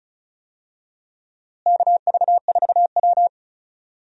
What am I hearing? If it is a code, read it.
Morse code, heard as a keyed tone, about 36 wpm, 700 Hz. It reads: KV4W
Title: USA Call Signs - 4 Characters - Repeated 3x - 35wpm